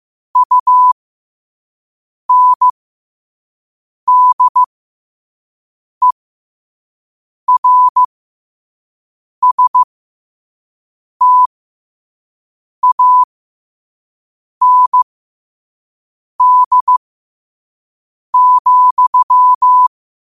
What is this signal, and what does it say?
Morse code, heard as a keyed tone, about 15 wpm, spaced as for 5 wpm, 1000 Hz. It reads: UNDERSTAND,
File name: qotdfiles / 5wpm